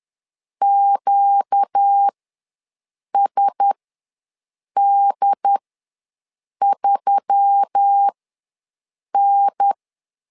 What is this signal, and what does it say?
Morse code, heard as a keyed tone, about 10 wpm, spaced as for 6 wpm, 800 Hz. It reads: QSD3N
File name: LEZIONI / 4 Q1S3 ws 10wpm 2437